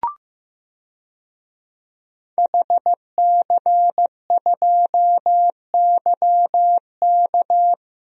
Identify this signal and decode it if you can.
Morse code, heard as a keyed tone, about 15 wpm, 700 Hz. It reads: HC2YK
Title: DXCC-100 Call Signs - 5 Characters - Repeated 2x - 15wpm